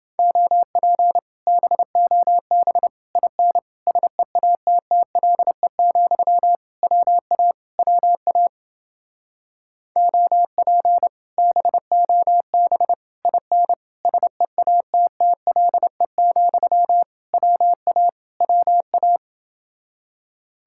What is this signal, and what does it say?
Morse code, heard as a keyed tone, about 28 wpm, 700 Hz. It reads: OP BOB IN SEATTLE, WA WA OP BOB IN SEATTLE, WA WA